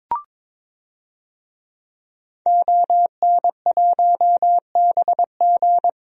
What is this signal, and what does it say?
Morse code, heard as a keyed tone, about 23 wpm, 700 Hz. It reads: ON1BG
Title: DXCC-100 Call Signs - 5 Characters - Repeated 3x - 22wpm